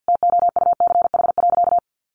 Morse code, heard as a keyed tone, about 49 wpm, 700 Hz. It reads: TOUCH.
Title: Sentences from Top 1500 Words 50wpm